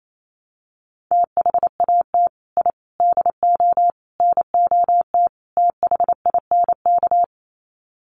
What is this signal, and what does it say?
Morse code, heard as a keyed tone, about 28 wpm, 700 Hz. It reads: THAT I DO NOT THINK
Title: Sets of 5 Words Encore 28wpm